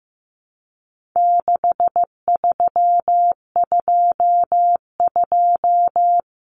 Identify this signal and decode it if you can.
Morse code, heard as a keyed tone, about 15 wpm, 700 Hz. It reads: 6322
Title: Numbers - 4-Digit 15wpm